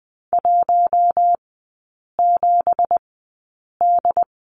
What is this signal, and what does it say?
Morse code, heard as a keyed tone, about 20 wpm, spaced as for 8 wpm, 690 Hz. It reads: 17D
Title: 191 Lesson 31e - Letter v - Three Characters 20wpm